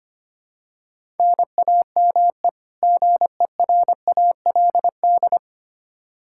Morse code, heard as a keyed tone, about 25 wpm, 700 Hz. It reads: NAME GERALD
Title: Practice QSO - Exchange 1 - Beginning - 25wpm